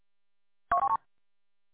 There is no code.